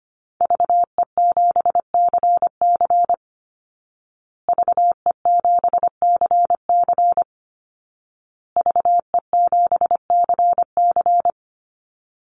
Morse code, heard as a keyed tone, about 25 wpm, 700 Hz. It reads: VE7CC VE7CC VE7CC
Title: Call Signs Repeated 2x - 25wpm